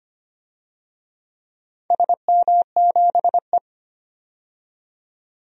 Morse code, heard as a keyed tone, about 25 wpm, 690 Hz. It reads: SM7E
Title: DXCC-100 Call Signs - 4 Characters - 25wpm